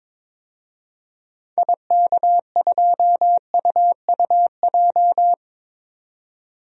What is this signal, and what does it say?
Morse code, heard as a keyed tone, about 22 wpm, 700 Hz. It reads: IK2UUJ
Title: DXCC-100 Call Signs - 6 Characters - 22wpm